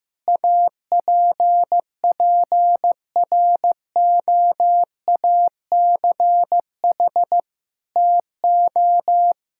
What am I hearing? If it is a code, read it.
Morse code, heard as a keyed tone, about 15 wpm, 700 Hz. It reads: APPROACH TO
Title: Sentences from Top 1000 Words - Review Part 6 - 15wpm